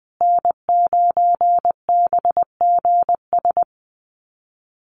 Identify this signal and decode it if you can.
Morse code, heard as a keyed tone, about 19 wpm, 700 Hz. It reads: N9BGS
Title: USA Call Signs Repeated 3x 20wpm